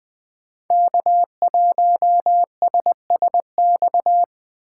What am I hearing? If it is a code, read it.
Morse code, heard as a keyed tone, about 20 wpm, 700 Hz. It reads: K1SSX
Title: Code Groups - 5 Letter:Numbers - 20wpm